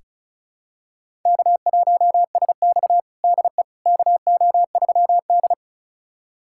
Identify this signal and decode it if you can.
Morse code, heard as a keyed tone, about 36 wpm, 700 Hz. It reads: K1SX DE KO3D
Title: USA Callsign DE Callsign - 4 Characters - 35wpm